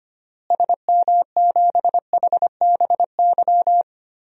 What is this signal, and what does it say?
Morse code, heard as a keyed tone, about 25 wpm, 700 Hz. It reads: SM7HBY
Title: DXCC-100 Call Signs - 6 Characters - Repeated 3x - 25wpm